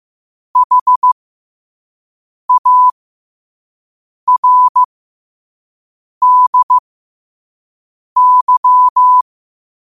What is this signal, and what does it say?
Morse code, heard as a keyed tone, about 15 wpm, spaced as for 5 wpm, 1000 Hz. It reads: HARDY
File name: qotdfiles / 5wpm